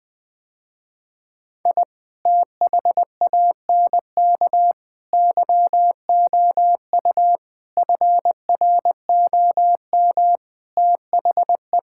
Morse code, heard as a keyed tone, about 20 wpm, 700 Hz. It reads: I THANK YOU FROM THE
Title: Sentences with 6-Word Phrases - Spoken Beginning - 20wpm